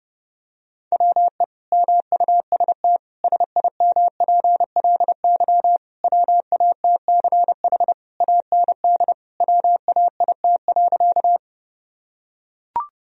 Morse code, heard as a keyed tone, about 30 wpm, 700 Hz. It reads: WE MUST SIMPLY WATCH AND WAIT.
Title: Sentences from Top 1000 Words - Review Part 4 - 30wpm